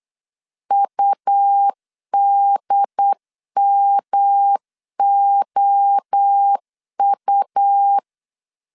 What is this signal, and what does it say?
Morse code, heard as a keyed tone, about 8 wpm, 800 Hz. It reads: UDMOU